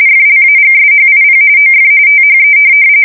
RTTY, about 45.45 baud. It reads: FLAGSHIP OF THE
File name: fsk170 / USS-New-Jersey